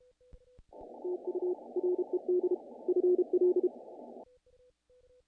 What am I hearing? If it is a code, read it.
Morse code, heard as a keyed tone, about 32 wpm, 360 Hz. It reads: TU RED FL
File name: contests / naqp-cw-k5tr-op-k5pi- -1800